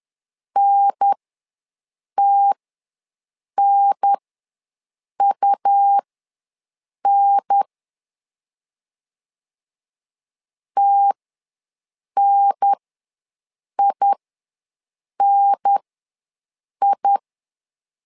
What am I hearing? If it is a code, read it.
Morse code, heard as a keyed tone, about 10 wpm, spaced as for 6 wpm, 800 Hz. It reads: NTNUN TNINI
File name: 2 UHDN ws 10wpm